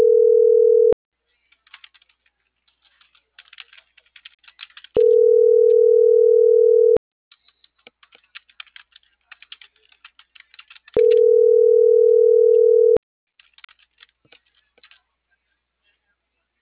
Ringback tone (North America), 440 + 480 Hz.